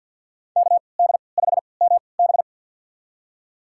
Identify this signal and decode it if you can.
Morse code, heard as a keyed tone, about 52 wpm, spaced as for 29 wpm, 700 Hz. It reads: KD5MB
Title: USA Call Signs - 5 Characters - 3x Character Spacing - 50wpm